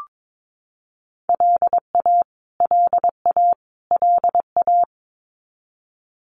Morse code, heard as a keyed tone, about 22 wpm, 700 Hz. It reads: LA LA LA
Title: US States Repeated 3x 22wpm